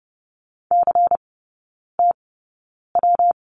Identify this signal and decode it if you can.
Morse code, heard as a keyed tone, about 30 wpm, spaced as for 9 wpm, 700 Hz. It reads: CTW
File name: Lesson 19e - Letter w - Three Characters 30wpm